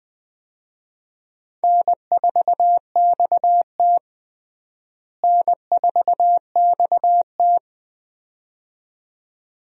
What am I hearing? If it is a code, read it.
Morse code, heard as a keyed tone, about 20 wpm, 700 Hz. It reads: N4XT N4XT
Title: USA Call Signs - 4 Characters - Repeated 2x - 20wpm